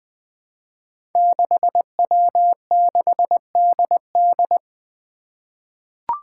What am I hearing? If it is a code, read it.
Morse code, heard as a keyed tone, about 20 wpm, 700 Hz. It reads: 6W6DD
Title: Five Challenging Letter-Number Combinations 20wpm